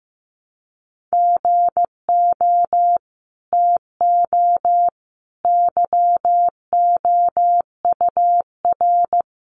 Morse code, heard as a keyed tone, about 15 wpm, 700 Hz. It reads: GO TO YOUR
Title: Sentences from Top 300 Words 15wpm